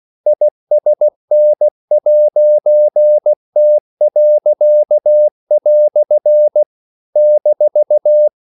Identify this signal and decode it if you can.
Morse code, heard as a keyed tone, about 16 wpm, 600 Hz. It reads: ISN'T." -
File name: CW / Practice / EnglishQuotations0416